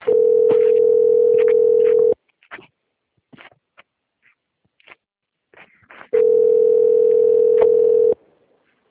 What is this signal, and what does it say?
Ringback tone (North America), 440 + 480 Hz.
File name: out- - - - - .19892